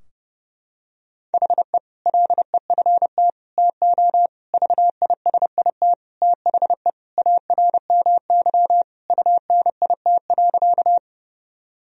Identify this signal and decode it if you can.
Morse code, heard as a keyed tone, about 30 wpm, 700 Hz. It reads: HE LEFT TO VISIT THE ARMY UNIT.
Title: Sentences from Top 1000 Words - Review Part 2 - 30wpm